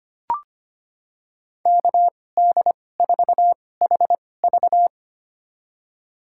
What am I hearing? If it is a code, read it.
Morse code, heard as a keyed tone, about 25 wpm, spaced as for 18 wpm, 700 Hz. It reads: KD4HV
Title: USA Call Signs - 5 Characters - 2x Character Spacing - 25wpm